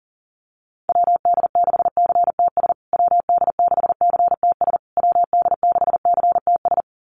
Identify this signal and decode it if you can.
Morse code, heard as a keyed tone, about 40 wpm, 700 Hz. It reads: WD6CTS WD6CTS WD6CTS
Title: USA Call Signs Repeated 3x 40wpm